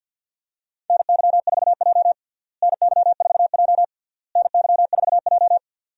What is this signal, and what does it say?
Morse code, heard as a keyed tone, about 47 wpm, 700 Hz. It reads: NY4J NY4J NY4J
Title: USA Call Signs - 4 Characters - Repeated 3x - 50wpm